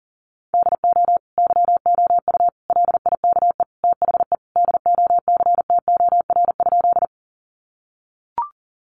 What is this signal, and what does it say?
Morse code, heard as a keyed tone, about 40 wpm, 700 Hz. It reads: DO YOU LIKE THE DOCTOR?
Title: Sentences from Top 1500 Words 40wpm